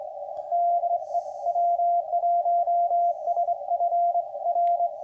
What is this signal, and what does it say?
Morse code, heard as a keyed tone, about 24 wpm, 690 Hz. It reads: DE W1SFR